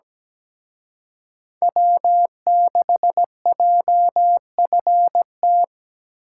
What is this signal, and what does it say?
Morse code, heard as a keyed tone, about 20 wpm, 700 Hz. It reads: W6JFT
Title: Speed Racing - USA 1x3 Call Signs 17wpm